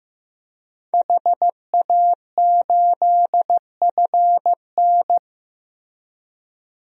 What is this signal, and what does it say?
Morse code, heard as a keyed tone, about 15 wpm, 700 Hz. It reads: HA8FN
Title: DXCC-100 Call Signs - 5 Characters - Repeated 3x - 15wpm